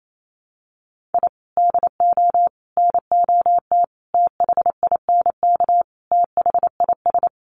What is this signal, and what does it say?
Morse code, heard as a keyed tone, about 28 wpm, 700 Hz. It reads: I DO NOT THINK THIS
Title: Sets of 5 Words Encore 28wpm